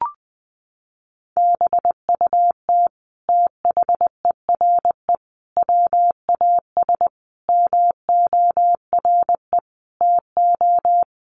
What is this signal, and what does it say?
Morse code, heard as a keyed tone, about 20 wpm, 700 Hz. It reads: BUT THERE WAS MORE TO
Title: Sentences from Top 100 Words 20wpm